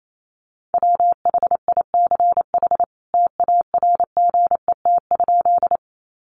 Morse code, heard as a keyed tone, about 28 wpm, 700 Hz. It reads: WHICH TARGET?
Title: Sentences from Top 1200 Words 28wpm